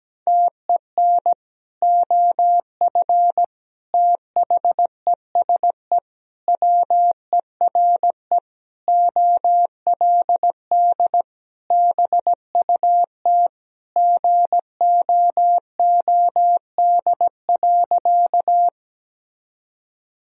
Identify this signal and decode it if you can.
Morse code, heard as a keyed tone, about 17 wpm, 700 Hz. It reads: TEN OF THESE WERE OLD BUT GOOD.
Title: Sentences from Top 1000 Words - Review Part 2 - 17wpm